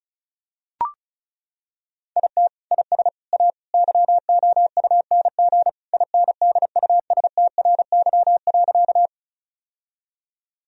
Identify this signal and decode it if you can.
Morse code, heard as a keyed tone, about 36 wpm, 700 Hz. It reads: IT IS A YOUNG INDUSTRY.
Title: Sentences from Top 1000 Words - Review Part 4 - 35wpm